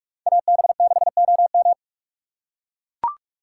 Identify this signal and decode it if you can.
Morse code, heard as a keyed tone, about 43 wpm, 700 Hz. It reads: AB6OM
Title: USA Call Signs - 5 Characters - 4x Character Spacing - 45wpm